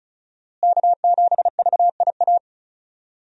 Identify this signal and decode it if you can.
Morse code, heard as a keyed tone, about 36 wpm, 700 Hz. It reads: K7VIA
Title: USA Call Signs - 5 Characters - Repeated 2x - 35wpm